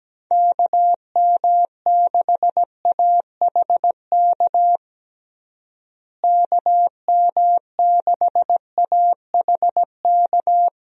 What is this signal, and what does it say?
Morse code, heard as a keyed tone, about 17 wpm, 700 Hz. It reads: KM6AHK KM6AHK
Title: USA Call Signs - 6 Characters - Repeated 3x - 17wpm